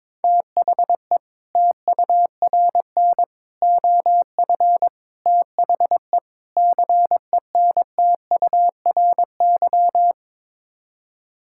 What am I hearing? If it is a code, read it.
Morse code, heard as a keyed tone, about 22 wpm, 700 Hz. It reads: THE TURN OF THE CENTURY
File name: practice / 22wpm / Build A Phrase - 5-Word Phrases - 22wpm